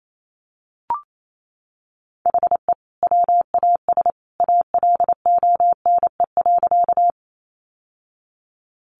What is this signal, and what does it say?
Morse code, heard as a keyed tone, about 28 wpm, 700 Hz. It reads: HE WAS ALONE.
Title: Sentences from Top 1000 Words 28wpm